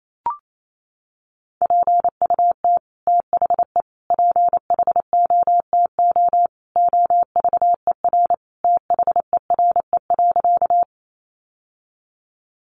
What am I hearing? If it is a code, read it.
Morse code, heard as a keyed tone, about 28 wpm, 700 Hz. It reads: PUT THE PHOTO OVER THERE.
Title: Sentences from Top 1200 Words 28wpm